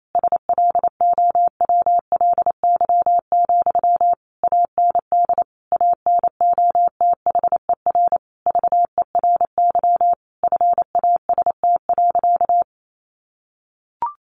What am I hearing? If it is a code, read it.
Morse code, heard as a keyed tone, about 28 wpm, 700 Hz. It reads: SLOWLY, AND ANOTHER VERY FAST.
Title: Sentences from Top 1100 Words 28wpm